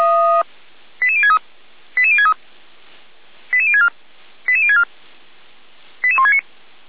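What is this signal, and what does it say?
Five-tone selective call (ZVEI), heard as quick runs of five tones, tones of about 70 ms. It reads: 88063 88063 88064 88064 88169
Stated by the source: ZVEI five-tone selective call received by SDR, from the Feuerwehr Orlamünde audio server.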